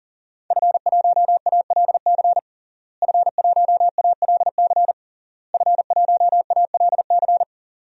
Morse code, heard as a keyed tone, about 40 wpm, 700 Hz. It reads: F1ALC F1ALC F1ALC
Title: DXCC-100 Call Signs - 5 Characters - Repeated 3x - 40wpm